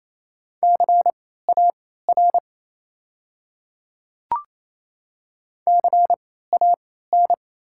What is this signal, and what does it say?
Morse code, heard as a keyed tone, about 27 wpm, spaced as for 16 wpm, 690 Hz. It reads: CAR CAN
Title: 3x Character Spacing - Top 300 Words - 28wpm